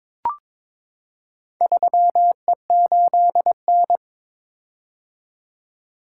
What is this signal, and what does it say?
Morse code, heard as a keyed tone, about 22 wpm, 700 Hz. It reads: 3E8N
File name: practice / 22wpm / Four Letter-Number Combinations - 22wpm